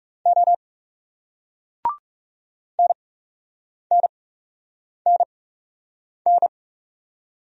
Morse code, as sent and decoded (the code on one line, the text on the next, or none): --- / -. -. -. -.
O NNNN